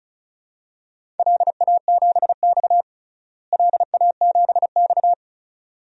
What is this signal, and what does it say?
Morse code, heard as a keyed tone, about 36 wpm, 700 Hz. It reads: LA7X LA7X
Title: DXCC-100 Call Signs - 4 Characters - Repeated 2x - 35wpm